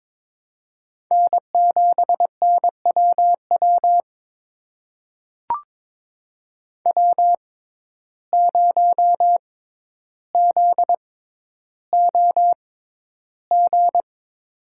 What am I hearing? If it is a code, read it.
Morse code, heard as a keyed tone, about 22 wpm, spaced as for 11 wpm, 700 Hz. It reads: N7NWW W0ZOG
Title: USA Call Signs - 5 Characters - 6x Character Spacing - 22wpm